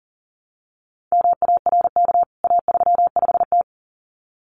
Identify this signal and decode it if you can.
Morse code, heard as a keyed tone, about 38 wpm, 700 Hz. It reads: MARK A35T
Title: CWT Exchange 40wpm